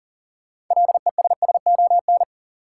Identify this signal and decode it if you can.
Morse code, heard as a keyed tone, about 40 wpm, 700 Hz. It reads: LESSON
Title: Sentences from Top 1200 Words - New Word Drill - 40wpm